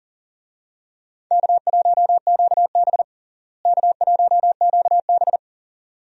Morse code, heard as a keyed tone, about 37 wpm, 700 Hz. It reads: K1QB K1QB
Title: USA Call Signs - 4 Characters - Repeated 2x - 40wpm